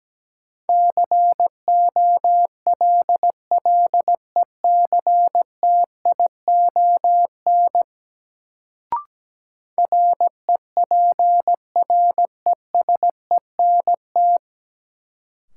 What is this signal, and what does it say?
Morse code, heard as a keyed tone, about 17 wpm, 700 Hz. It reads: COLLECTION REPRESENT
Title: Sentences from Top 1200 Words - New Word Drill - 17wpm